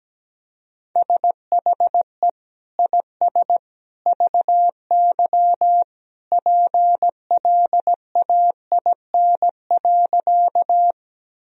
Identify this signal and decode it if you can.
Morse code, heard as a keyed tone, about 17 wpm, 700 Hz. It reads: SHE IS VY PLAIN.
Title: Sentences from Top 400 Words and Abbreviations 17wpm